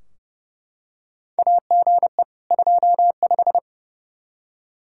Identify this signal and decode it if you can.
Morse code, heard as a keyed tone, about 30 wpm, 700 Hz. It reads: AGE 25
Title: Practice QSO - Exchange 3 - More Info - 30wpm